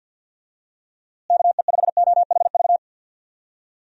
Morse code, heard as a keyed tone, about 52 wpm, 700 Hz. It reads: KE5OHV
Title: USA Call Signs - 6 Characters - Repeated 3x - 50wpm